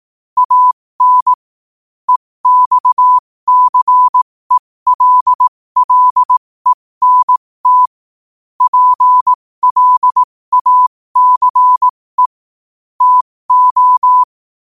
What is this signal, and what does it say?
Morse code, heard as a keyed tone, about 18 wpm, spaced as for 15 wpm, 1000 Hz. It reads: AN EXCELLENT PLACE TO
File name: qotdfiles / 15wpm